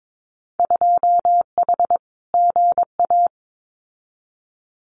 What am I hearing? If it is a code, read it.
Morse code, heard as a keyed tone, about 22 wpm, 700 Hz. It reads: 2H GA
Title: Winter Field Day Exchanges 22wpm